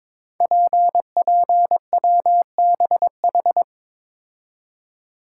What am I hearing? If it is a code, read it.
Morse code, heard as a keyed tone, about 22 wpm, 700 Hz. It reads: PPWBH
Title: Code Groups - 5 Letters - 22wpm